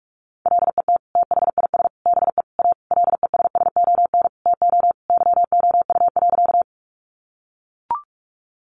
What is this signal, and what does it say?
Morse code, heard as a keyed tone, about 44 wpm, 700 Hz. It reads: LET THIS BE A LESSON TO YOU.
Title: Sentences from Top 1200 Words 45wpm